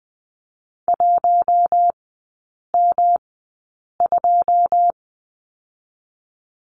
Morse code, heard as a keyed tone, about 20 wpm, spaced as for 8 wpm, 700 Hz. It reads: 1M2
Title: 138 Lesson 23e - Letter f - Three Characters 20wpm